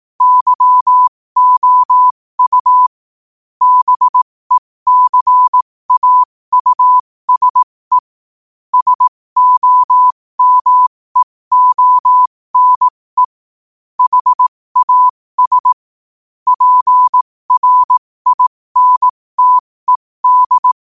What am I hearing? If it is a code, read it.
Morse code, heard as a keyed tone, about 18 wpm, spaced as for 15 wpm, 1000 Hz. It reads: YOU BECAUSE SOMEONE HAS PRINTED
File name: qotdfiles / 15wpm